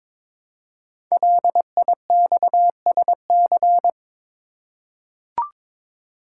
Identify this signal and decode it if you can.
Morse code, heard as a keyed tone, about 23 wpm, 700 Hz. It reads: LIXSC
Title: Code Groups - 5 Letters - 22wpm